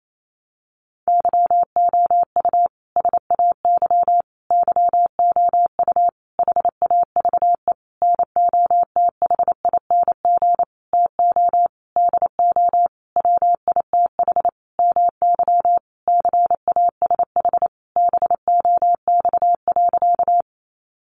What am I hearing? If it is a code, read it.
Morse code, heard as a keyed tone, about 28 wpm, 700 Hz. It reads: YOU SAY YOU HAVE NOTHING TO DO WITH MY CASH BOX.
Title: Sentences from Top 700 Words 28wpm